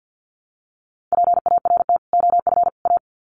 Morse code, heard as a keyed tone, about 50 wpm, 700 Hz. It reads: PART OF A
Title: Sets of 3 Words 50wpm